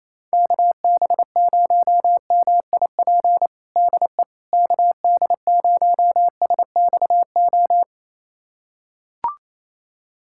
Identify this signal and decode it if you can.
Morse code, heard as a keyed tone, about 28 wpm, 700 Hz. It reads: KB0MIP DE KD0SXO
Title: USA Callsign DE Callsign 28wpm